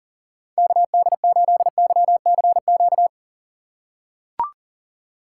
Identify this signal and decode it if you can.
Morse code, heard as a keyed tone, about 40 wpm, 700 Hz. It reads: KD8YCQ
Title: USA Call Signs - 3x Character Spacing - 40wpm